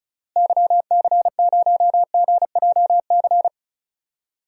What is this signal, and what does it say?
Morse code, heard as a keyed tone, about 36 wpm, 700 Hz. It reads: YC0GJC
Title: DXCC-100 Call Signs - 6 Characters - Repeated 3x - 35wpm